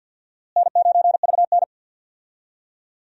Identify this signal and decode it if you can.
Morse code, heard as a keyed tone, about 50 wpm, 700 Hz. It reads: N9VN
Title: USA Call Signs - 4 Characters - Repeated 3x - 50wpm